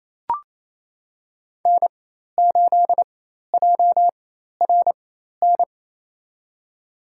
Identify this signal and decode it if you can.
Morse code, heard as a keyed tone, about 28 wpm, spaced as for 13 wpm, 700 Hz. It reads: N8JRN
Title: USA Call Signs - 5 Characters - 4x Character Spacing - 28wpm